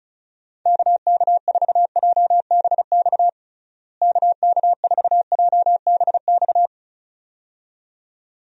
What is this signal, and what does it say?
Morse code, heard as a keyed tone, about 36 wpm, 700 Hz. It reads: KK4JBX KK4JBX
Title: USA Call Signs - 6 Characters - Repeated 2x - 35wpm